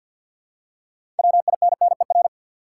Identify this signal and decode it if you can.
Morse code, heard as a keyed tone, about 52 wpm, 700 Hz. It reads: WINNER
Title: Six Letter Words - 50wpm